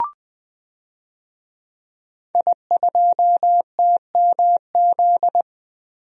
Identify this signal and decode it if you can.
Morse code, heard as a keyed tone, about 21 wpm, 700 Hz. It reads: I2TMZ
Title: DXCC-100 Call Signs - 5 Characters - Repeated 3x - 20wpm